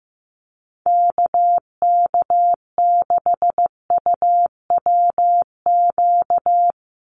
Morse code, heard as a keyed tone, about 15 wpm, 700 Hz. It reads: KK6UWQ
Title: USA Call Signs 15wpm